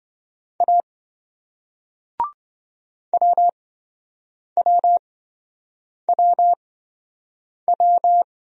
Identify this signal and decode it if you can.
Morse code, heard as a keyed tone, about 26 wpm, spaced as for 7 wpm, 700 Hz. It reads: A WWWW